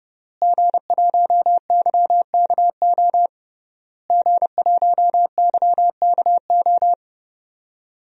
Morse code, heard as a keyed tone, about 32 wpm, 700 Hz. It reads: G1YKO G1YKO
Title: DXCC-100 Call Signs - 5 Characters - Repeated 3x - 30wpm